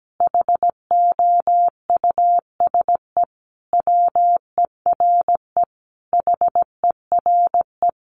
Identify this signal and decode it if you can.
Morse code, heard as a keyed tone, about 17 wpm, 700 Hz. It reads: HOUSE WERE HERE
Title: Sentences from Top 1400 Words 17wpm